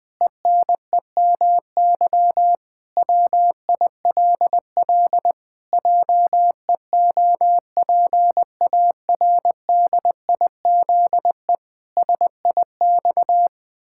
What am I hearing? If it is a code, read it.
Morse code, heard as a keyed tone, about 20 wpm, 700 Hz. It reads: ENEMY WILL JEOPARDIZE SIX
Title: Pangram Challenge v2 - 20wpm